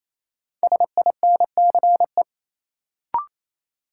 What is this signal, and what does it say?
Morse code, heard as a keyed tone, about 28 wpm, 700 Hz. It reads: SINCE
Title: Sentences from Top 300 Words - New Word Drill - 28wpm